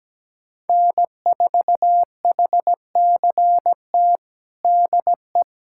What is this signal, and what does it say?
Morse code, heard as a keyed tone, about 17 wpm, 700 Hz. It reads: N4HCT DE
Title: USA Callsign DE Callsign - 5 Characters - 17wpm